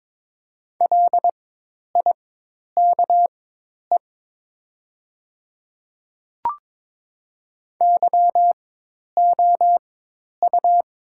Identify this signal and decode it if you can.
Morse code, heard as a keyed tone, about 22 wpm, spaced as for 10 wpm, 700 Hz. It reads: LIKE YOU
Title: 4x Character Spacing - Top 100 Words - 22wpm